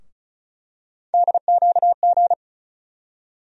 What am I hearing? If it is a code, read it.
Morse code, heard as a keyed tone, about 34 wpm, 700 Hz. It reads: DQG
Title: USA Call Sign Postfixes - 3 Letters 35wpm